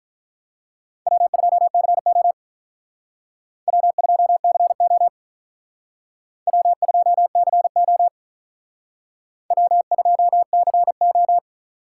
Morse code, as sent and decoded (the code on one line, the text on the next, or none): .-- ..--- -.-. --- / .-- ..--- -.-. --- / .-- ..--- -.-. --- / .-- ..--- -.-. ---
W2CO W2CO W2CO W2CO